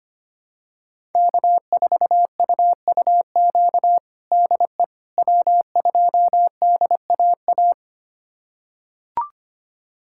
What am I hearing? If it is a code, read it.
Morse code, heard as a keyed tone, about 25 wpm, 700 Hz. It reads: K4UUQ DE W2DAA
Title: USA Callsign DE Callsign - 5 Characters - 25wpm